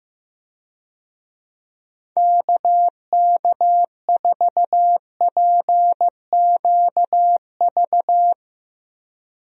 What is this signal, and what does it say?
Morse code, heard as a keyed tone, about 16 wpm, 700 Hz. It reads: KK4PQV
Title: USA Call Signs - 6 Characters - Repeated 2x - 15wpm